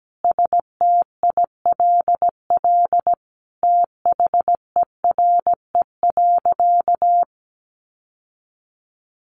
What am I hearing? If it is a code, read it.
Morse code, heard as a keyed tone, about 17 wpm, 700 Hz. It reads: STILL THERE.
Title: Sentences from Top 1500 Words 17wpm